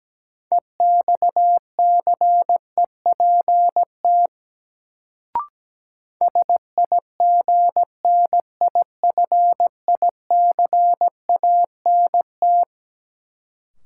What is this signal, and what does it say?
Morse code, heard as a keyed tone, about 17 wpm, 700 Hz. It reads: EXCEPT SIGNIFICANT
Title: Sentences from Top 1600 Words - New Word Drill Alt - 17wpm